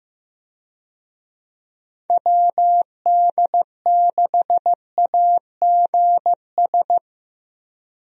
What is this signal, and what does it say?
Morse code, heard as a keyed tone, about 15 wpm, 700 Hz. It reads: WD6AGS
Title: USA Call Signs - 6 Characters - Repeated 3x - 15wpm